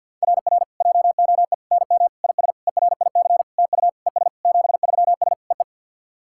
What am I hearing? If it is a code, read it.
Morse code, heard as a keyed tone, about 48 wpm, 700 Hz. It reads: RR JOE NM IS ERIC TU ES 73S EE
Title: SOTA QSO Practice 50wpm